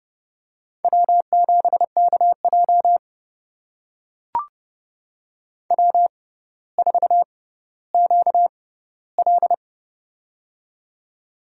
Morse code, heard as a keyed tone, about 30 wpm, spaced as for 15 wpm, 700 Hz. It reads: W7KJ W4QL
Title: USA Call Signs - 6x Character Spacing - 30wpm